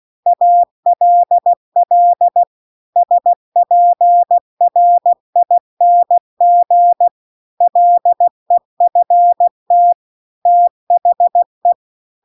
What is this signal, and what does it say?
Morse code, heard as a keyed tone, about 16 wpm, 700 Hz. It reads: ALL SPRING LEFT THE